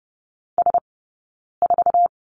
Morse code, heard as a keyed tone, about 30 wpm, spaced as for 9 wpm, 690 Hz. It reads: S4
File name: course / 30:10wpm / 086 Lesson 15d - Number 5 - Two Characters 30wpm